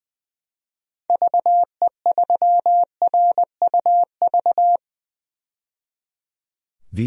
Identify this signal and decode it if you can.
Morse code, heard as a keyed tone, about 20 wpm, 700 Hz. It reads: VE3RUV
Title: DXCC-100 Call Signs - 6 Characters - 20wpm